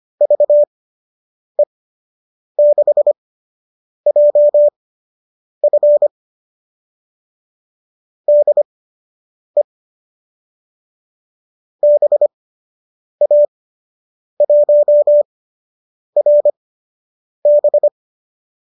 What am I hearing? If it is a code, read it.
Morse code, heard as a keyed tone, about 25 wpm, spaced as for 8 wpm, 590 Hz. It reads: VE6JF DE BA1RB